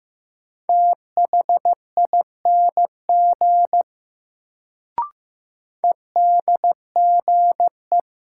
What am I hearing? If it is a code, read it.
Morse code, heard as a keyed tone, about 15 wpm, 700 Hz. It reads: THING EDGE
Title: Sets of 3 Words - New Word Drill - 15wpm